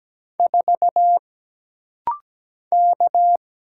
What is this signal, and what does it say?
Morse code, heard as a keyed tone, about 17 wpm, 700 Hz. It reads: 4 K
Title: Single Letter-Number Alt - Rapid-Fire 17wpm